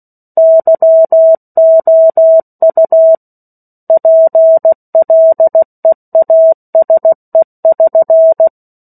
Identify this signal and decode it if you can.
Morse code, heard as a keyed tone, about 16 wpm, 650 Hz. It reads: YOU PLEASE<SN>
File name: 03-Chapter-03 16 WPM